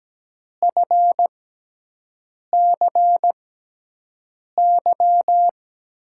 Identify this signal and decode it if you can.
Morse code, heard as a keyed tone, about 17 wpm, spaced as for 6 wpm, 700 Hz. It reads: FCY
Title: USA Call Signs - 6 Characters - 6x Character Spacing - 17wpm